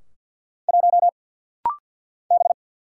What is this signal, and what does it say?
Morse code, heard as a keyed tone, about 50 wpm, 700 Hz. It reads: 1 B
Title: Single Letter-Number - Rapid-Fire 50wpm